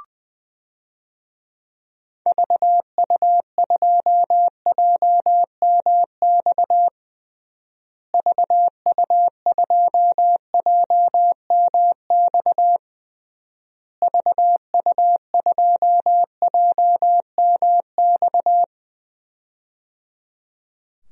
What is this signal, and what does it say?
Morse code, heard as a keyed tone, about 20 wpm, 700 Hz. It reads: VU2JMX VU2JMX VU2JMX
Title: DXCC-100 Call Signs - 6 Characters - Repeated 3x - 20wpm